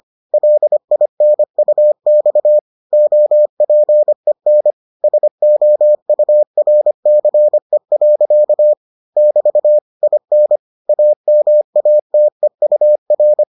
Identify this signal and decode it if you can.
Morse code, heard as a keyed tone, about 25 wpm, 600 Hz. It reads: LINUX OPEN SOURCE. <BT> IN AMATEUR